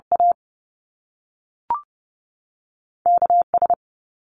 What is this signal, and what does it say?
Morse code, heard as a keyed tone, about 30 wpm, spaced as for 6 wpm, 700 Hz. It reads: AKS
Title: US States 30wpm